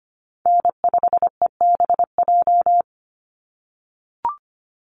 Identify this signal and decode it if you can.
Morse code, heard as a keyed tone, about 25 wpm, 700 Hz. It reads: N5EBJ
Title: USA Call Signs 25wpm